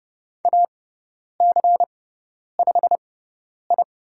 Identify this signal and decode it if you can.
Morse code, heard as a keyed tone, about 30 wpm, spaced as for 10 wpm, 700 Hz. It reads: AC5I
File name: Lesson 16g - Letter c - Call Signs 30wpm